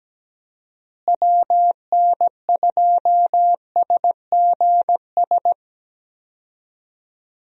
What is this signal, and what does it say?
Morse code, heard as a keyed tone, about 17 wpm, 700 Hz. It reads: WN2SGS
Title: USA Call Signs - 6 Characters - Repeated 3x - 17wpm